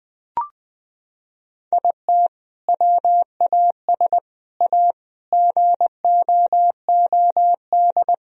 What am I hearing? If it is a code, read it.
Morse code, heard as a keyed tone, about 21 wpm, 700 Hz. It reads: IT WAS A GOOD
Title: Sentences from Top 1000 Words - Review Part 1 - 20wpm